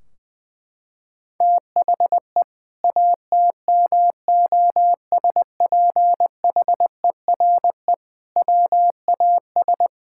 Morse code, heard as a keyed tone, about 19 wpm, 700 Hz. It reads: THE ATMOSPHERE WAS
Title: Sentences from Top 1600 Words 20wpm